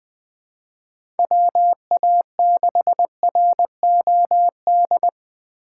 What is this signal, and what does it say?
Morse code, heard as a keyed tone, about 20 wpm, 700 Hz. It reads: WA6ROD
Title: USA Call Signs - 4x Character Spacing - 20wpm